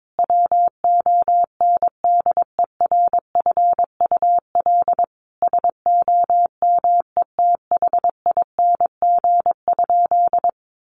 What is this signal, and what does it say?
Morse code, heard as a keyed tone, about 22 wpm, 700 Hz. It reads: WONDERFUL SOMETHING?
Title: Sentences from Top 1100 Words 22wpm